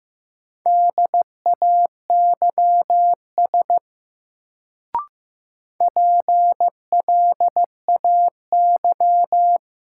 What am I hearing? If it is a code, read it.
Morse code, heard as a keyed tone, about 15 wpm, 700 Hz. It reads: DAYS PLAY
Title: Sets of 5 Words Encore - New Word Drill - 15wpm